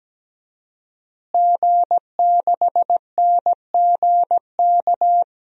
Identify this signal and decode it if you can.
Morse code, heard as a keyed tone, about 17 wpm, 700 Hz. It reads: G6NGK
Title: DXCC-100 Call Signs - 5 Characters - Repeated 3x - 17wpm